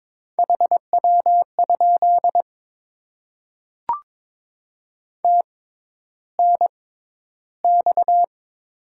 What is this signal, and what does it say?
Morse code, heard as a keyed tone, about 22 wpm, spaced as for 11 wpm, 700 Hz. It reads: HW? TNX
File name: Top 20 QSO Elements - 6x Character Spacing - 22wpm